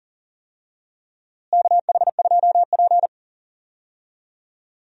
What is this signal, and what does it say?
Morse code, heard as a keyed tone, about 39 wpm, 700 Hz. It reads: KH2P
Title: USA Call Signs - 4 Characters - Repeated 2x - 40wpm